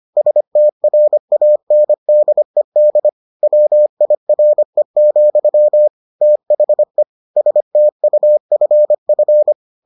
Morse code, heard as a keyed tone, about 25 wpm, 600 Hz. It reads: STRANDED WIRE, THE STUFF